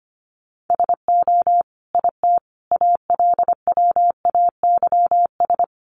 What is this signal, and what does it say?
Morse code, heard as a keyed tone, about 25 wpm, 700 Hz. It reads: SO IT ALWAYS
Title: Sentences from Top 300 Words 25wpm